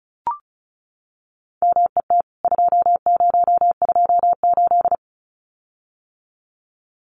Morse code, heard as a keyed tone, about 34 wpm, 700 Hz. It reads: MET 2028
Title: CWT Exchange 35wpm